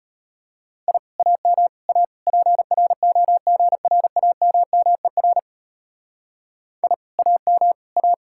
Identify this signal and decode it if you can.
Morse code, heard as a keyed tone, about 29 wpm, 700 Hz. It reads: I AM A PROGRAMMER I AM A
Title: Speed-Racing Common QSO Phrases Encore - 25wpm